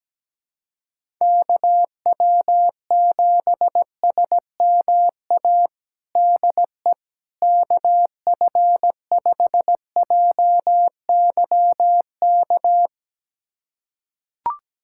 Morse code, heard as a keyed tone, about 17 wpm, 700 Hz. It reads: KW7SMA DE KF5JYK
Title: USA Callsign DE Callsign 17wpm